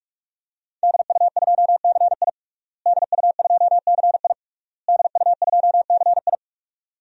Morse code, heard as a keyed tone, about 46 wpm, 700 Hz. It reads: DU2CI DU2CI DU2CI
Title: DXCC-100 Call Signs - 5 Characters - Repeated 3x - 45wpm